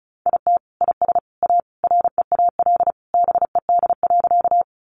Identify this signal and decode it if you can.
Morse code, heard as a keyed tone, about 34 wpm, 700 Hz. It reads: IT IS A REAL BED.
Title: Sentences from Top 500 Words 35wpm